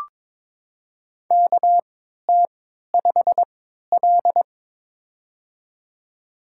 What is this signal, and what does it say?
Morse code, heard as a keyed tone, about 22 wpm, spaced as for 12 wpm, 700 Hz. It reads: KT5L
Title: USA Call Signs - 3x Character Spacing - 22wpm